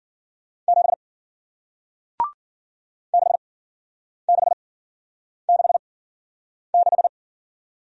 Morse code, heard as a keyed tone, about 53 wpm, spaced as for 9 wpm, 700 Hz. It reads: 7 6666